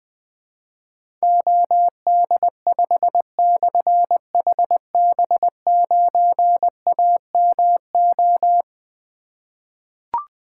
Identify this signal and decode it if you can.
Morse code, heard as a keyed tone, about 20 wpm, 700 Hz. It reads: OD5/HB9AMO
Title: Difficult Call Signs 20wpm